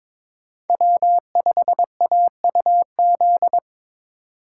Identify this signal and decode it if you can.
Morse code, heard as a keyed tone, about 28 wpm, 700 Hz. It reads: W5AUZ